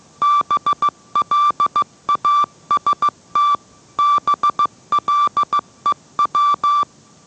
Morse code, heard as a keyed tone, about 15 wpm, 1170 Hz. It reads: BLAST BLEW